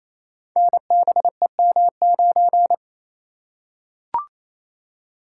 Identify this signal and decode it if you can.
Morse code, heard as a keyed tone, about 29 wpm, 700 Hz. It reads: NBEM9
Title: Code Groups - 5 Letter:Numbers - 28wpm